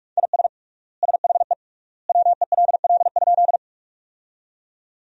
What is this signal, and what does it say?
Morse code, heard as a keyed tone, about 46 wpm, 700 Hz. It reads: IS SHE WELL?
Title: 3x Word-Spacing - Sentences from Top 200 Words - 45wpm